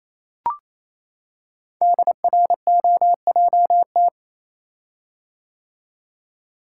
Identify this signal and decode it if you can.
Morse code, heard as a keyed tone, about 27 wpm, 700 Hz. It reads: DROJT
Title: Code Groups - 5 Letters - 28wpm